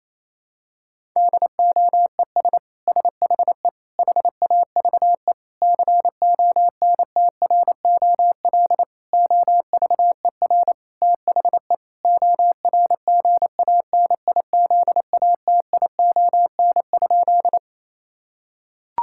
Morse code, heard as a keyed tone, about 28 wpm, 700 Hz. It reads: DOES SHE HAVE CONTROL OVER THE ORGANIZATION?
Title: Sentences from Top 1000 Words - Review Part 5 - 28wpm